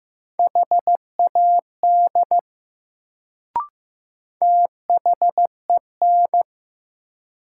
Morse code, heard as a keyed tone, about 15 wpm, 700 Hz. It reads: HAD THEN
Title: Sentences from Top 100 Words - New Word Drill - 15wpm